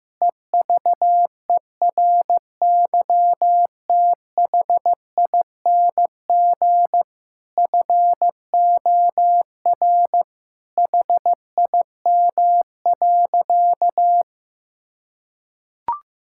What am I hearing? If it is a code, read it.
Morse code, heard as a keyed tone, about 15 wpm, 700 Hz. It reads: EVERYTHING FOR HIM.
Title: Sentences from Top 1000 Words - Review Part 6 - 15wpm